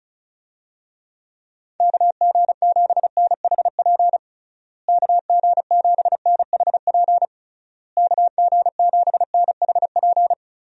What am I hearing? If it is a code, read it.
Morse code, heard as a keyed tone, about 34 wpm, 700 Hz. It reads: KG7NHP KG7NHP KG7NHP
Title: USA Call Signs - 6 Characters - Repeated 3x - 35wpm